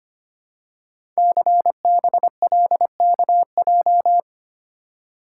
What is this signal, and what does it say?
Morse code, heard as a keyed tone, about 25 wpm, 700 Hz. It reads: CBLKJ
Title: Code Groups - 5 Letters - 25wpm